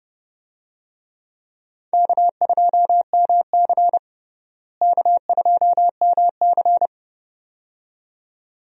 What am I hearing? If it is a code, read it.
Morse code, heard as a keyed tone, about 32 wpm, 700 Hz. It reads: K2MC K2MC
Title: USA Call Signs - 4 Characters - Repeated 2x - 30wpm